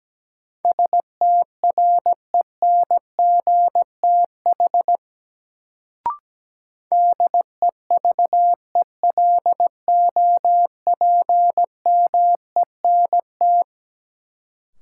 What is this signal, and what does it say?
Morse code, heard as a keyed tone, about 17 wpm, 700 Hz. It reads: STRENGTH DEVELOPMENT
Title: Sentences from Top 1000 Words - Review Part 4 - New Word Drill - 17wpm